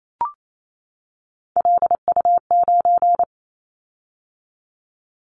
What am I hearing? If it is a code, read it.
Morse code, heard as a keyed tone, about 28 wpm, 700 Hz. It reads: LU9
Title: Callsign Prefixes for Top 50 DXCC - 28wpm